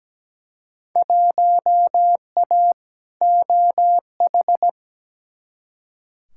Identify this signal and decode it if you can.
Morse code, heard as a keyed tone, about 17 wpm, 700 Hz. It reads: 1A OH
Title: Field Day Exchanges - 17wpm